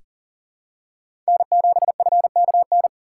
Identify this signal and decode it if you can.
Morse code, heard as a keyed tone, about 40 wpm, 700 Hz. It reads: N7FKN
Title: USA Call Signs - 6x Character Spacing - 40wpm